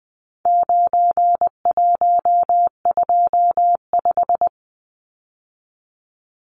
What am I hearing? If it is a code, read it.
Morse code, heard as a keyed tone, about 20 wpm, 700 Hz. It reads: 9125
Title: Numbers - 4-Digit 20wpm